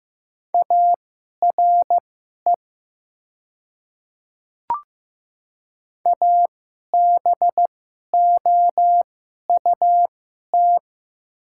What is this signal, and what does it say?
Morse code, heard as a keyed tone, about 15 wpm, spaced as for 11 wpm, 700 Hz. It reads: ARE ABOUT
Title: 2x Character Spacing - Top 200 Words - 15wpm